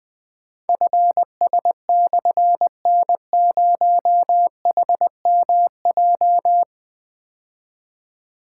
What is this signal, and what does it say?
Morse code, heard as a keyed tone, about 21 wpm, 700 Hz. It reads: FS/N0HMJ
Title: Difficult Call Signs 20wpm